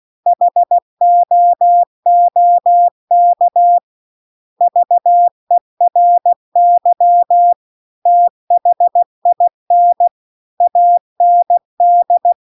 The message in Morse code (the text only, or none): HOOK VERY THIN AND